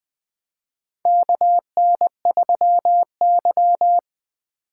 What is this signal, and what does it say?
Morse code, heard as a keyed tone, about 20 wpm, 700 Hz. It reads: KN3Y
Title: 2023 SOTA Chaser Call Signs - 20wpm